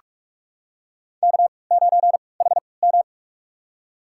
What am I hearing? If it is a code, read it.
Morse code, heard as a keyed tone, about 46 wpm, spaced as for 26 wpm, 700 Hz. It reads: K9HM
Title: USA Call Signs - 3x Character Spacing - 45wpm